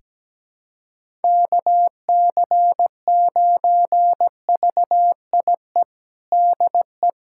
Morse code, heard as a keyed tone, about 17 wpm, 700 Hz. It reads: KC9VIE DE
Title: USA Callsign DE Callsign - 6 Characters - 17wpm